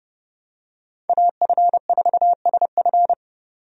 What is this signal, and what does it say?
Morse code, heard as a keyed tone, about 30 wpm, 700 Hz. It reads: AF4SF
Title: USA Call Signs - 5 Characters - 30wpm